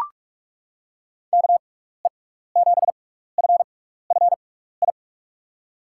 Morse code, heard as a keyed tone, about 42 wpm, spaced as for 15 wpm, 700 Hz. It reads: KE7FFI
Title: USA Call Signs - 6x Character Spacing - 45wpm